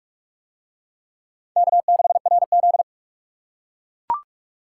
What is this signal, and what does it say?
Morse code, heard as a keyed tone, about 39 wpm, 700 Hz. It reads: K6RZ